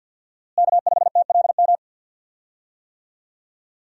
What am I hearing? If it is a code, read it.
Morse code, heard as a keyed tone, about 47 wpm, 700 Hz. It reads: K5TLM
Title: USA Call Signs - 5 Characters - Repeated 3x - 50wpm